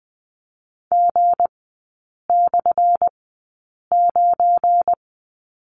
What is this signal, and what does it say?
Morse code, heard as a keyed tone, about 20 wpm, spaced as for 8 wpm, 700 Hz. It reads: G/9
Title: 184 Lesson 30e - Letter b - Three Characters 20wpm